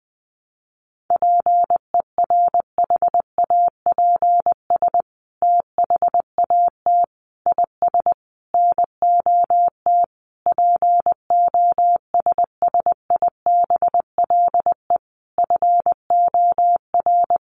Morse code, heard as a keyed tone, about 20 wpm, 700 Hz. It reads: PERHAPS THAT IS NOT POSSIBLE FOR
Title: Sentences from Top 900 Words 20wpm